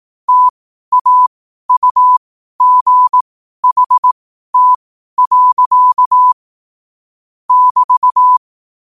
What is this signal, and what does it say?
Morse code, heard as a keyed tone, about 18 wpm, spaced as for 12 wpm, 1000 Hz. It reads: TAUGHT. <BT>
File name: qotdfiles / 12wpm